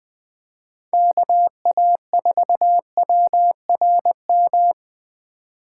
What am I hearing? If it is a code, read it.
Morse code, heard as a keyed tone, about 20 wpm, 700 Hz. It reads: KA4WRM